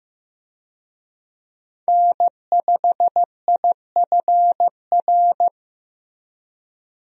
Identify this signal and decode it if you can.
Morse code, heard as a keyed tone, about 15 wpm, 700 Hz. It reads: N5IFR
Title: USA Call Signs - 5 Characters - 15wpm